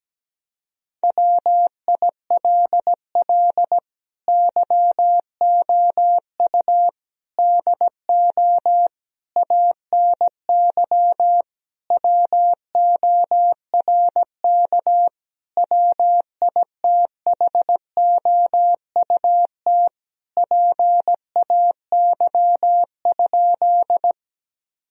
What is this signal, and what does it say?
Morse code, heard as a keyed tone, about 17 wpm, 700 Hz. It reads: WILL YOU DO ANY WORK WITHOUT PAY?
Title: Sentences from Top 1000 Words - Review Part 3 - 17wpm